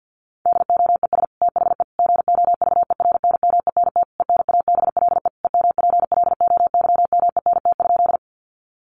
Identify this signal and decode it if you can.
Morse code, heard as a keyed tone, about 50 wpm, 700 Hz. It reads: DOES THE GOVERNMENT ENABLE EMPLOYMENT?
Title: Sentences from Top 1500 Words 50wpm